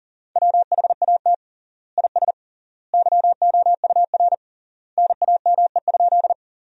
Morse code, heard as a keyed tone, about 40 wpm, 700 Hz. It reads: WHAT IS YOUR NAME?
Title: 3x Word-Spacing - Sentences from Top 200 Words - 40wpm